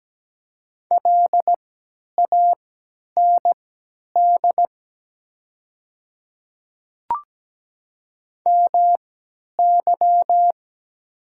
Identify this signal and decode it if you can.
Morse code, heard as a keyed tone, about 17 wpm, spaced as for 10 wpm, 700 Hz. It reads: LAND MY
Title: 3x Character Spacing - Top 300 Words - 17wpm